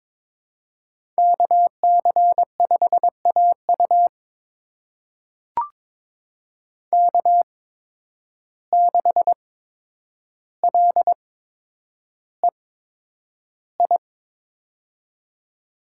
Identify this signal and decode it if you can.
Morse code, heard as a keyed tone, about 22 wpm, spaced as for 9 wpm, 700 Hz. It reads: KC5AU K6LEI
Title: USA Call Signs - 5 Characters - 8x Character Spacing - 22wpm